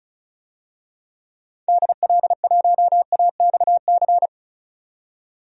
Morse code, heard as a keyed tone, about 36 wpm, 700 Hz. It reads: DL1AXC
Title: DXCC-100 Call Signs - 6 Characters - 35wpm